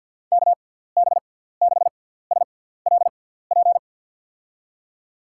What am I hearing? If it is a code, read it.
Morse code, heard as a keyed tone, about 54 wpm, spaced as for 17 wpm, 700 Hz. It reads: KB6SLP
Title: USA Call Signs - 6x Character Spacing - 50wpm